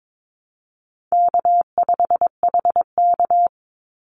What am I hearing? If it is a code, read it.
Morse code, heard as a keyed tone, about 22 wpm, 700 Hz. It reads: K5HK
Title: Easy Call Signs - 22wpm